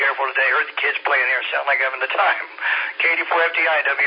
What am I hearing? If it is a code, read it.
Emergency Alert System (SAME) tones, heard as harsh buzzing bursts, with speech. It is heard from 0.5 to 3.2 s.